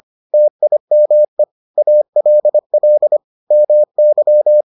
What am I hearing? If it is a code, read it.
Morse code, heard as a keyed tone, about 25 wpm, 600 Hz. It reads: TIME ALL MY